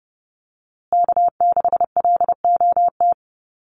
Morse code, heard as a keyed tone, about 30 wpm, 700 Hz. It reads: K6LOT
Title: USA Call Signs Repeated 2x 30wpm